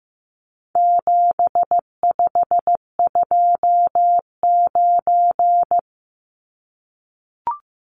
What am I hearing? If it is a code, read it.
Morse code, heard as a keyed tone, about 15 wpm, 700 Hz. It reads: 7529
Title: Numbers - 4-Digit 15wpm